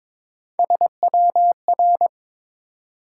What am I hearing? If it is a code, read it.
Morse code, heard as a keyed tone, about 22 wpm, 700 Hz. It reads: SWR
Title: Top 180 QSO Elements 22wpm